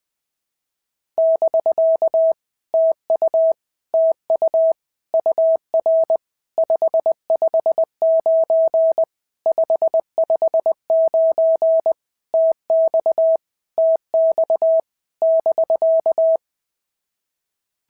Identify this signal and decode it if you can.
Morse code, heard as a keyed tone, about 20 wpm, 650 Hz. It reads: <BK> TU TU UR 559 559 TX TX <BK>
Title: POTA QSO Practice - 20wpm